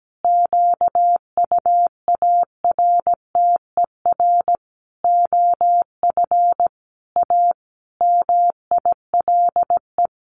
Morse code, heard as a keyed tone, about 17 wpm, 700 Hz. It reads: QUARTER OF A MILE
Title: Sentences from Top 1300 Words 17wpm